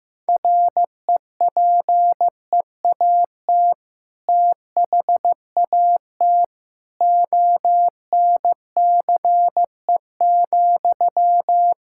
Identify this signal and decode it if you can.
Morse code, heard as a keyed tone, about 15 wpm, 700 Hz. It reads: REPEAT THAT ONCE,
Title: Sentences from Top 1600 Words 15wpm